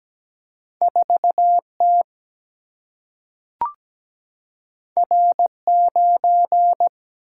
Morse code, heard as a keyed tone, about 17 wpm, 700 Hz. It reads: 4 T R 9
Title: Single-Letter Single-Number 17wpm